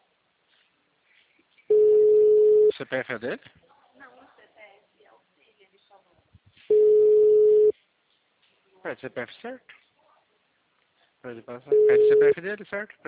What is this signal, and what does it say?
Ringback tone (Europe, 425 Hz), with speech.
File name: out- -2003